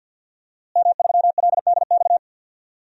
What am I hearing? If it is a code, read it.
Morse code, heard as a keyed tone, about 52 wpm, 700 Hz. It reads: M3LDX
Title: DXCC-100 Call Signs - 5 Characters - Repeated 2x - 50wpm